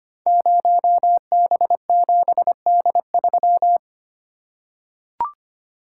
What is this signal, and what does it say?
Morse code, heard as a keyed tone, about 25 wpm, 700 Hz. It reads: 0B7D3
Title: Code Groups - 5 Letter:Numbers - 25wpm